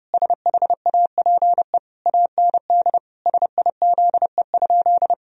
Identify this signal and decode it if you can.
Morse code, heard as a keyed tone, about 32 wpm, 700 Hz. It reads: SHAPE AND SIZE?
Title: Sentences from Top 1000 Words - Review Part 3 - 30wpm